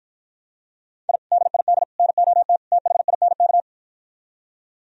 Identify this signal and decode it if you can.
Morse code, heard as a keyed tone, about 52 wpm, 700 Hz. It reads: I DID NOT THINK